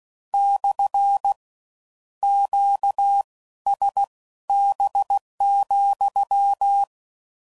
Morse code, heard as a keyed tone, about 16 wpm, spaced as for 13 wpm, 800 Hz. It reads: / QSB,
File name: CW Code Groups @15WPM / Q Calls / Q Calls SPSL2